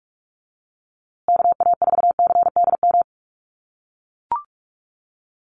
Morse code, heard as a keyed tone, about 44 wpm, 700 Hz. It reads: KA4CDM
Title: USA Call Signs Repeated 2x 45wpm